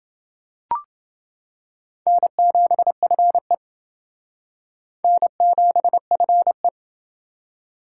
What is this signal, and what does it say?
Morse code, heard as a keyed tone, about 29 wpm, 700 Hz. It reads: N7FE N7FE